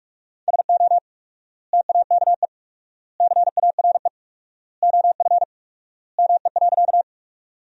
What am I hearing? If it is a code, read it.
Morse code, heard as a keyed tone, about 47 wpm, 700 Hz. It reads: SO TAKE CARE OF ME.
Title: 4x Word-Spacing - Sentences from Top 300 Words - 45wpm